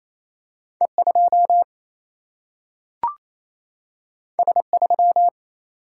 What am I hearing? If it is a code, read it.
Morse code, heard as a keyed tone, about 28 wpm, 700 Hz. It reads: E2 S3
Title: Single-Letter Single-Number 28wpm